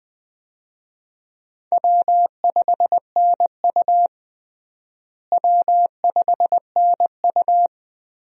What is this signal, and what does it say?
Morse code, heard as a keyed tone, about 21 wpm, 700 Hz. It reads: W5NU W5NU
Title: USA Call Signs - 4 Characters - Repeated 2x - 20wpm